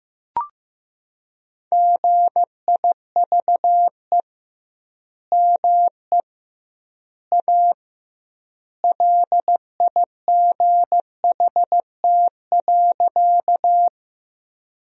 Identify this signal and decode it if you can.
Morse code, heard as a keyed tone, about 15 wpm, 700 Hz. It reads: GIVE ME A LIGHT.
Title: 2x Word-Spacing - Sentences from Top 200 Words - 15wpm